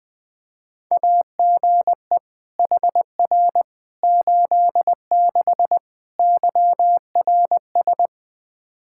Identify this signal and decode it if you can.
Morse code, heard as a keyed tone, about 20 wpm, 700 Hz. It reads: AGE HR 86 YRS
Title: Practice QSO - Exchange 3 - More Info - 20wpm